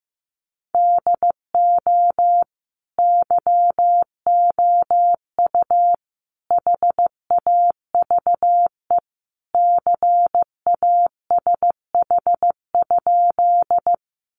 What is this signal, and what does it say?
Morse code, heard as a keyed tone, about 15 wpm, 700 Hz. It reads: DO YOU HAVE CASH?
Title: Sentences from Top 700 Words 15wpm